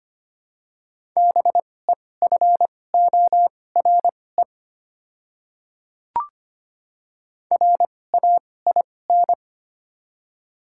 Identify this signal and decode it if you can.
Morse code, heard as a keyed tone, about 25 wpm, spaced as for 18 wpm, 700 Hz. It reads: BEFORE RAIN
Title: 2x Character Spacing - Top 300 Words - 25wpm